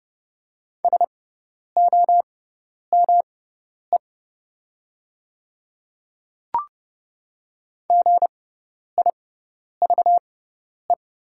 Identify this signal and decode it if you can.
Morse code, heard as a keyed tone, about 30 wpm, spaced as for 10 wpm, 700 Hz. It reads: SOME GIVE
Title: 6x Character Spacing - Top 300 Words - 30wpm